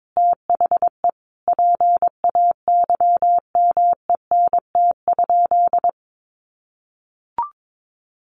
Morse code, heard as a keyed tone, about 22 wpm, 700 Hz. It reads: THE PAYMENT?
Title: Sentences from Top 1400 Words 22wpm